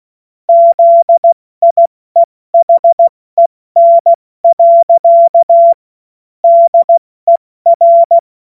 Morse code, heard as a keyed tone, about 16 wpm, 680 Hz. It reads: ZIEHEN. DER
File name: Morsekurs / Witze / Witz 07. Dezember 2021, 70 F80